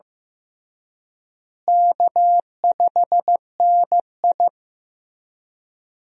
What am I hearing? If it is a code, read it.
Morse code, heard as a keyed tone, about 15 wpm, 700 Hz. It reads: K5NI
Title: USA Call Signs - 4 Characters - Repeated 3x - 15wpm